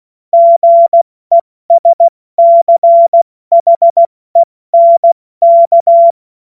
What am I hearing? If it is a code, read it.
Morse code, heard as a keyed tone, about 16 wpm, 680 Hz. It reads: GESCHENK
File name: Morsekurs / Witze / Witz 16. Mai 2023 70 F80